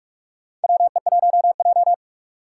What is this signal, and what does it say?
Morse code, heard as a keyed tone, about 41 wpm, 700 Hz. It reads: WE1J
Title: USA Call Signs - 4 Characters - 4x Character Spacing - 45wpm